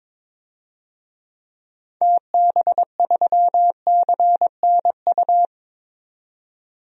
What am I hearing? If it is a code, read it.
Morse code, heard as a keyed tone, about 22 wpm, 700 Hz. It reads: TB3CNU
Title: DXCC-100 Call Signs - 6 Characters - 22wpm